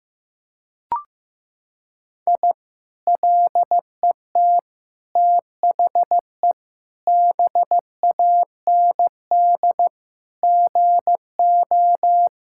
Morse code, heard as a keyed tone, about 15 wpm, 700 Hz. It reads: I LET THE BAND GO
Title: Sentences from Top 1600 Words 15wpm